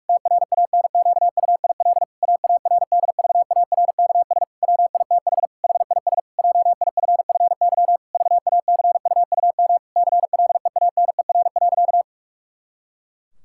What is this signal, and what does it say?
Morse code, heard as a keyed tone, about 46 wpm, 700 Hz. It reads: TRANQUIL AARDVARKS WITH HIS JIFFY VACUUM CLEANER.
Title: Pangram Challenge v2 - 45wpm